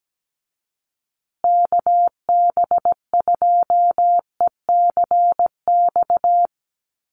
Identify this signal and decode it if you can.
Morse code, heard as a keyed tone, about 17 wpm, 700 Hz. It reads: KB2ECX
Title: USA Call Signs Repeated 3x 17wpm